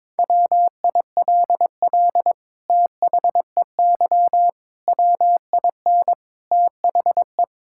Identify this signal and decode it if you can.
Morse code, heard as a keyed tone, about 22 wpm, 700 Hz. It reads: WILL THEY WIN THE
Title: Sentences from Top 1600 Words 22wpm